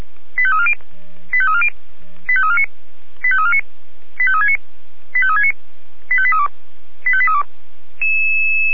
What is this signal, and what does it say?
Five-tone selective call (ZVEI), heard as quick runs of five tones, tones of about 70 ms. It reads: 85340 85340 85349 85349 85369 85369 85832 85832